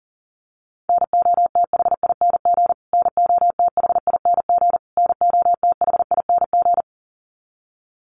Morse code, heard as a keyed tone, about 40 wpm, 700 Hz. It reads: NOTHING NOTHING NOTHING
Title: Top 500 Words Repeated 40wpm